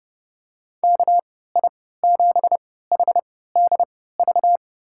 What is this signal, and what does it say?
Morse code, heard as a keyed tone, about 30 wpm, spaced as for 17 wpm, 700 Hz. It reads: KI7HDV
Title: USA Call Signs - 6 Characters - 3x Character Spacing - 30wpm